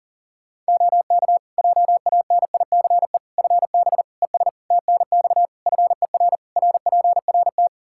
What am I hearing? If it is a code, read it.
Morse code, heard as a keyed tone, about 40 wpm, 700 Hz. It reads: OK JANICE FB ES TNX FER RPRT
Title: K1USN SST QSO Practice 40wpm